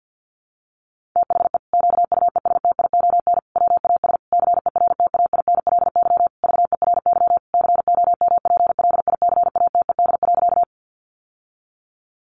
Morse code, heard as a keyed tone, about 50 wpm, 700 Hz. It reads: THE QUESTION WAS CERTAINLY VERY COMPLICATED.
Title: Sentences from Top 1200 Words 50wpm